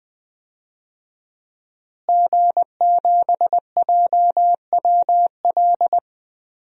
Morse code, heard as a keyed tone, about 20 wpm, 700 Hz. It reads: G7JWL
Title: DXCC-100 Call Signs - 5 Characters - Repeated 3x - 20wpm